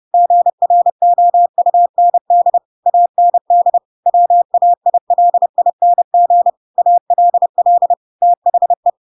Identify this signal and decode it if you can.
Morse code, heard as a keyed tone, about 30 wpm, 700 Hz. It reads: GROUND AND WAILING ALL THE
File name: Pinocchio / Pinocchio ch5 30 30 ik1hge